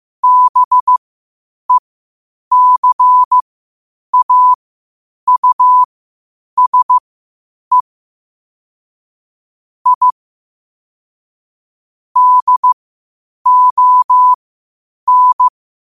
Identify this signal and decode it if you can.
Morse code, heard as a keyed tone, about 15 wpm, spaced as for 8 wpm, 1000 Hz. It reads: BECAUSE I DON
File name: qotdfiles / 8wpm